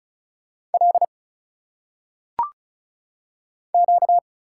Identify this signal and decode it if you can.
Morse code, heard as a keyed tone, about 35 wpm, 700 Hz. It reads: L Q
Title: Single Letters - Phonetic Pronunciation 35wpm